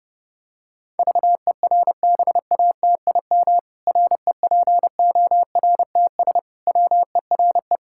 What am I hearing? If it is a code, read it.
Morse code, heard as a keyed tone, about 30 wpm, 700 Hz. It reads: VERBATIM REPORTS WERE
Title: Pangram Challenge v2 - 30wpm